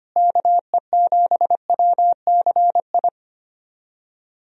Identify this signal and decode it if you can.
Morse code, heard as a keyed tone, about 25 wpm, 700 Hz. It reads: KE7WCI
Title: USA Call Signs - 6 Characters - Repeated 3x - 25wpm